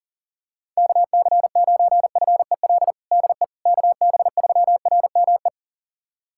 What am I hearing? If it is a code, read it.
Morse code, heard as a keyed tone, about 40 wpm, 700 Hz. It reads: KC9FEL DE KB3RME
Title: USA Callsign DE Callsign - 6 Characters - 40wpm